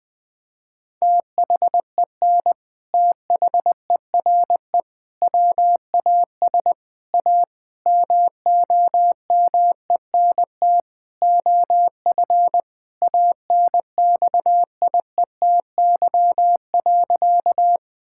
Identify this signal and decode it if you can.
Morse code, heard as a keyed tone, about 21 wpm, 700 Hz. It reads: THEN THERE WAS A MOMENT OF ANXIETY.
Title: Sentences from Top 1600 Words 20wpm